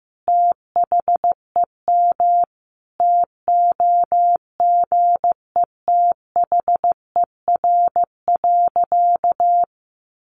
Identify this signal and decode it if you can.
Morse code, heard as a keyed tone, about 15 wpm, 700 Hz. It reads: THEM TOGETHER.
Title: Sentences from Top 1300 Words 15wpm